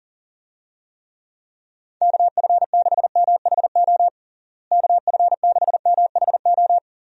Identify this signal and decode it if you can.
Morse code, heard as a keyed tone, about 40 wpm, 700 Hz. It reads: KF6MHO KF6MHO
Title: USA Call Signs - 6 Characters - Repeated 2x - 40wpm